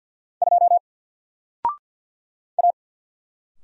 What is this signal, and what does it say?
Morse code, heard as a keyed tone, about 48 wpm, 700 Hz. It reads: 2 A
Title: Single Character (Letter:Number:Punctuation:Prosign) 50wpm